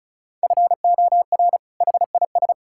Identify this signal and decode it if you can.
Morse code, heard as a keyed tone, about 34 wpm, 700 Hz. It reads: FOR HIS
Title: Sentences from Top 1000 Words - Review Part 5 - 35wpm